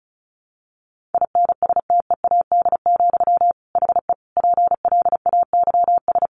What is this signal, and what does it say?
Morse code, heard as a keyed tone, about 34 wpm, 700 Hz. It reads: INSTEAD, HE PLAYS
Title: Additional Easy Phrases 35wpm